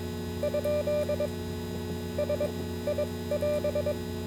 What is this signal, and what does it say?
Morse code, heard as a keyed tone, about 22 wpm, spaced as for 15 wpm, 590 Hz, with noise. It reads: ? SI<AS>